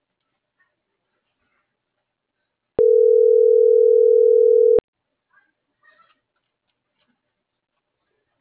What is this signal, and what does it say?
Ringback tone (North America), 440 + 480 Hz.